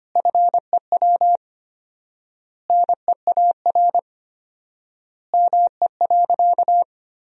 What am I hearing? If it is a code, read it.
Morse code, heard as a keyed tone, about 25 wpm, 700 Hz. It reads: FEW NEAR ME.
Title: 4x Word-Spacing - Sentences from Top 300 Words - 25wpm